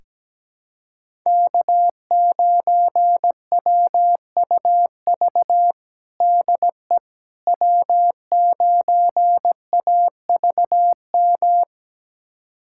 Morse code, heard as a keyed tone, about 17 wpm, 700 Hz. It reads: K9WUV DE W9AVM
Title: USA Callsign DE Callsign - 5 Characters - 17wpm